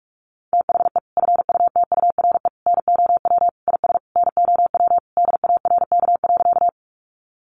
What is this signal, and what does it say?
Morse code, heard as a keyed tone, about 44 wpm, 700 Hz. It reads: THE FUTURE NOW IS NOW DARK.
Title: Sentences from Top 700 Words 45wpm